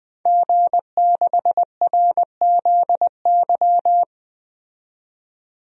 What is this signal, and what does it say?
Morse code, heard as a keyed tone, about 20 wpm, 700 Hz. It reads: G6RZY
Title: DXCC-100 Call Signs - 5 Characters - 20wpm